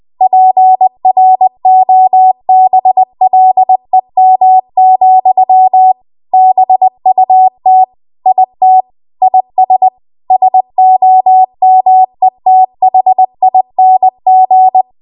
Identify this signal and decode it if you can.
Morse code, heard as a keyed tone, about 20 wpm, 750 Hz. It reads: PROBLEM, BUT IT IS SOMETHING
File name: Archive / 20 WPM / 20WPM